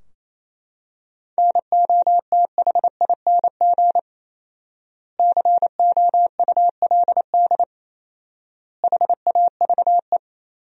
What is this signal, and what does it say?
Morse code, heard as a keyed tone, about 28 wpm, 700 Hz. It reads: NOTHING COULD HAVE
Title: 4x Word-Spacing - Sentences from Top 500 Words - 28wpm